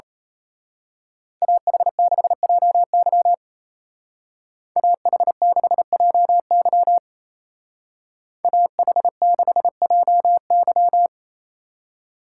Morse code, heard as a keyed tone, about 34 wpm, 700 Hz. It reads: AH6JY AH6JY AH6JY
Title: Speed Racing - USA Call Signs 28wpm